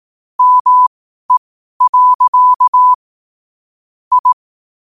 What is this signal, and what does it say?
Morse code, heard as a keyed tone, about 18 wpm, spaced as for 12 wpm, 1000 Hz. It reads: ME. I
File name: qotdfiles / 12wpm